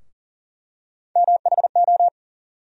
Morse code, heard as a keyed tone, about 40 wpm, 700 Hz. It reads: MHO
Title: USA Call Sign Postfixes - 3 Letters 40wpm